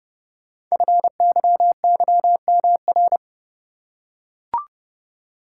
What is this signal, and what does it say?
Morse code, heard as a keyed tone, about 30 wpm, 700 Hz. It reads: FYYMR
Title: Code Groups - 5 Letter:Numbers - 30wpm